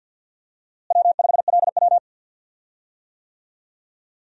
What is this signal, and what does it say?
Morse code, heard as a keyed tone, about 51 wpm, 700 Hz. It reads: W5LW
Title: USA Call Signs - 4 Characters - Repeated 3x - 50wpm